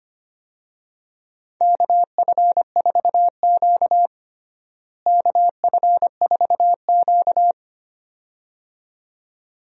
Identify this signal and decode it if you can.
Morse code, heard as a keyed tone, about 25 wpm, 700 Hz. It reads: KF4Q KF4Q
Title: USA Call Signs - 4 Characters - Repeated 2x - 25wpm